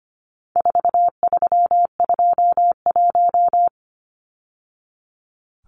Morse code, heard as a keyed tone, about 25 wpm, 700 Hz. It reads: 4321
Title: Numbers - 4-Digit 25wpm